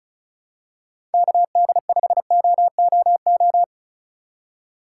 Morse code, heard as a keyed tone, about 34 wpm, 700 Hz. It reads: KD5OOO
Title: USA Call Signs - 6 Characters - 2x Character Spacing - 35wpm